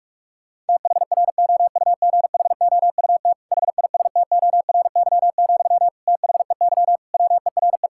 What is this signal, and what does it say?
Morse code, heard as a keyed tone, about 46 wpm, 700 Hz. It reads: THROUGHOUT HISTORY, THEY WERE
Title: Sentences from Top 1000 Words - Review Part 5 - 45wpm